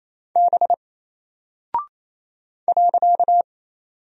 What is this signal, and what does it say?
Morse code, heard as a keyed tone, about 27 wpm, 700 Hz. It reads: B .
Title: Single Character (Letter:Number:Punctuation:Prosign) 28wpm